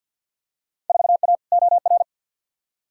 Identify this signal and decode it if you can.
Morse code, heard as a keyed tone, about 53 wpm, 700 Hz. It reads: 4A OR
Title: Field Day Exchanges Encore - 50wpm